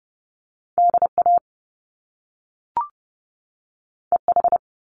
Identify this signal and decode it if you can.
Morse code, heard as a keyed tone, about 30 wpm, 700 Hz. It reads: DA EH
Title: Top Two Letter Combinations in Words 30wpm